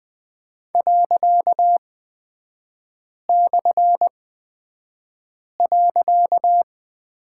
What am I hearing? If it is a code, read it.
Morse code, heard as a keyed tone, about 20 wpm, spaced as for 5 wpm, 700 Hz. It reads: ./.
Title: 255 Lesson 41e - Period - Three Characters 20wpm